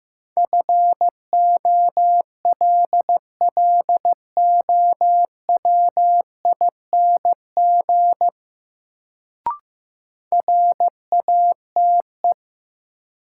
Morse code, heard as a keyed tone, about 15 wpm, 700 Hz. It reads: FOLLOWING RATE
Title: Sentences from Top 700 Words - New Word Drill - 15wpm